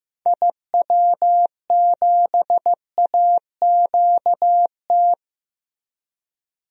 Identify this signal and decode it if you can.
Morse code, heard as a keyed tone, about 15 wpm, 700 Hz. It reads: IW7AQT
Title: DXCC-100 Call Signs - 6 Characters - Repeated 2x - 15wpm